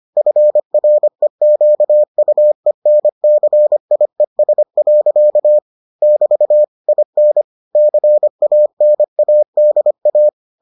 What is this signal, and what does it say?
Morse code, heard as a keyed tone, about 25 wpm, 600 Hz. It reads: FREQUENCIES. <BT> IN CANADA